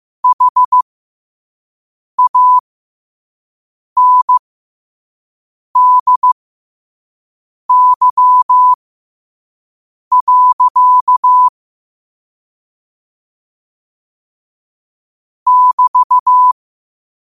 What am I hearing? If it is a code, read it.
Morse code, heard as a keyed tone, about 15 wpm, spaced as for 5 wpm, 1000 Hz. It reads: HANDY. <BT>